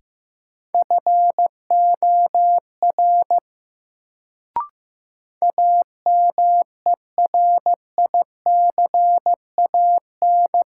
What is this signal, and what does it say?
Morse code, heard as a keyed tone, about 15 wpm, 700 Hz. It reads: FOR AMERICAN
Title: Sets of 2 Words - New Word Drill - 15wpm